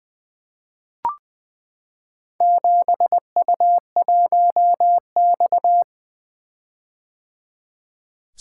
Morse code, heard as a keyed tone, about 20 wpm, 700 Hz. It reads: 7U1X
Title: Four Letter-Number Combinations - 20wpm